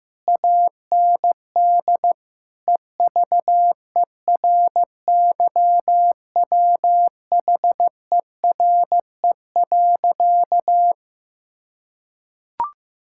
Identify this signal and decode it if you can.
Morse code, heard as a keyed tone, about 15 wpm, 700 Hz. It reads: AND EVERYWHERE.
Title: Sentences from Top 1600 Words 15wpm